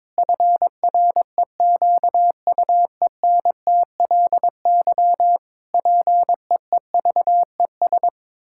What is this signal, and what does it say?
Morse code, heard as a keyed tone, about 22 wpm, 700 Hz. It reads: FREQUENTLY PEEVES
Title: Pangram Challenge v2 - 22wpm